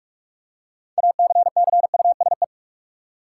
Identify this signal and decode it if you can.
Morse code, heard as a keyed tone, about 46 wpm, 700 Hz. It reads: ACCUSE
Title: Six Letter Words - 45wpm